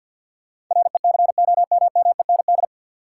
Morse code, heard as a keyed tone, about 46 wpm, 700 Hz. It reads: RECOMMEND
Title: Sentences from Top 1200 Words - New Word Drill Alt - 50wpm